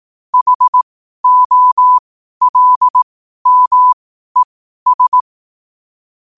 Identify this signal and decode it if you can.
Morse code, heard as a keyed tone, about 18 wpm, spaced as for 12 wpm, 1000 Hz. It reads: HOLMES
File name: qotdfiles / 12wpm